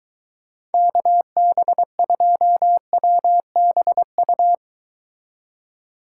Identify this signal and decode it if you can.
Morse code, heard as a keyed tone, about 23 wpm, 700 Hz. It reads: KB2WBU